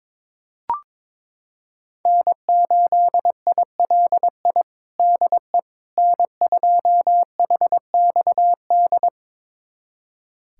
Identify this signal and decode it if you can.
Morse code, heard as a keyed tone, about 22 wpm, 700 Hz. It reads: N8ILI DE N2HXD
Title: USA Callsign DE Callsign - 5 Characters - 22wpm